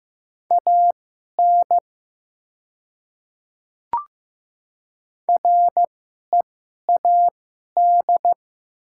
Morse code, heard as a keyed tone, about 15 wpm, spaced as for 11 wpm, 700 Hz. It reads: AN READ
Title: 2x Character Spacing - Top 500 Words - 15wpm